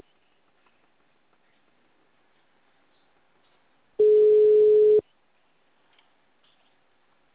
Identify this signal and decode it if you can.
Ringback tone (Europe, 425 Hz).
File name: out- -2012- - - .1630